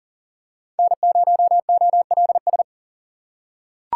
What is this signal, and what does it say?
Morse code, heard as a keyed tone, about 40 wpm, 700 Hz. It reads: N0OLS
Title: USA Call Signs - 6x Character Spacing - 40wpm